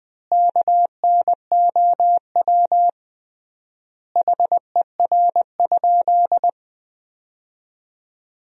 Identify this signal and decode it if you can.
Morse code, heard as a keyed tone, about 20 wpm, 700 Hz. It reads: KNOW HER?
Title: 3x Word-Spacing - Sentences from Top 100 Words - 20wpm